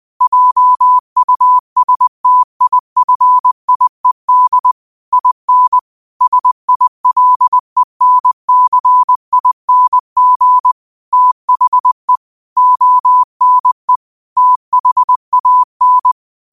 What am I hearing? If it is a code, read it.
Morse code, heard as a keyed tone, about 20 wpm, 1000 Hz. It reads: JUSTIFIED IN SILENCING THE ONE THAN
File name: qotdfiles / 20wpm